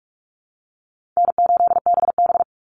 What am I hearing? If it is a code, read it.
Morse code, heard as a keyed tone, about 44 wpm, 700 Hz. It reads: N8BB
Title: NA SOTA Chaser Callsigns 45wpm